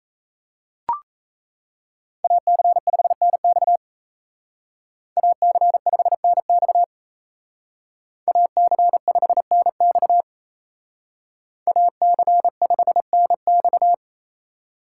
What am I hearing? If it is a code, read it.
Morse code, heard as a keyed tone, about 36 wpm, 700 Hz. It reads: AC5NX AC5NX AC5NX AC5NX